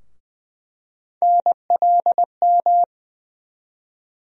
Morse code, heard as a keyed tone, about 20 wpm, 700 Hz. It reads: NLM
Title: USA Call Sign Postfixes - 3 Letters 20wpm